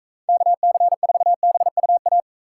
Morse code, heard as a keyed tone, about 42 wpm, 700 Hz. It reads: KC4BUA